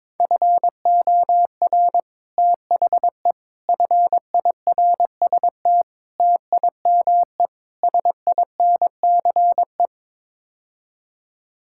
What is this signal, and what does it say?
Morse code, heard as a keyed tone, about 22 wpm, 700 Hz. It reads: FOR THE FIRST TIME SINCE
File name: practice / 22wpm / Build A Phrase - 5-Word Phrases - 22wpm